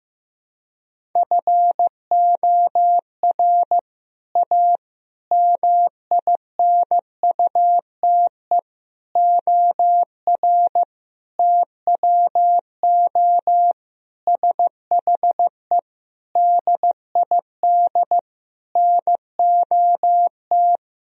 Morse code, heard as a keyed tone, about 15 wpm, 700 Hz. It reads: FOR A MINUTE OR TWO SHE DID NOT
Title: Sentences from Top 1000 Words - Review Part 1 - 15wpm